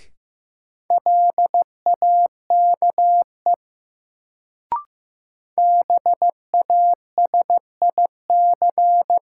Morse code, heard as a keyed tone, about 15 wpm, 700 Hz. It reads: LAKE BASIC
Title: Sentences from Top 1000 Words - Review Part 3 - New Word Drill - 15wpm